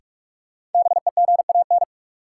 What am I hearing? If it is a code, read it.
Morse code, heard as a keyed tone, about 46 wpm, 700 Hz. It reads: BEGAN
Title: Sentences from Top 300 Words - New Word Drill - 45wpm